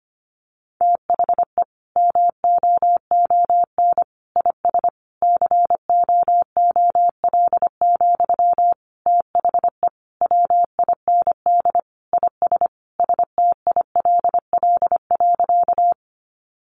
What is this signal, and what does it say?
Morse code, heard as a keyed tone, about 25 wpm, 700 Hz. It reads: THE MOON IS COOL, THE WIND IS STILL.
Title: Sentences from Top 1200 Words 25wpm